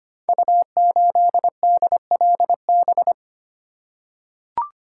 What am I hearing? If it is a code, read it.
Morse code, heard as a keyed tone, about 25 wpm, 700 Hz. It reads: U8DLB
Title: Code Groups - 5 Letter:Numbers - 25wpm